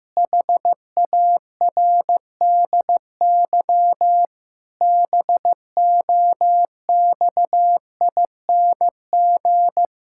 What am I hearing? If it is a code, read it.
Morse code, heard as a keyed tone, about 16 wpm, 700 Hz. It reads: HARDY BOXING
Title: Pangram Challenge v2 - 15wpm